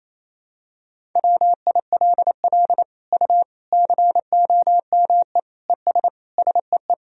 Morse code, heard as a keyed tone, about 28 wpm, 700 Hz. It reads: WILL U COME ES SEE
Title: Sentences from Top 100 Words and Abbreviations 28wpm